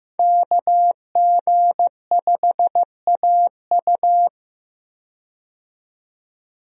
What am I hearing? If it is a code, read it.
Morse code, heard as a keyed tone, about 16 wpm, 700 Hz. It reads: KG5AU
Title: USA Call Signs - 5 Characters - 15wpm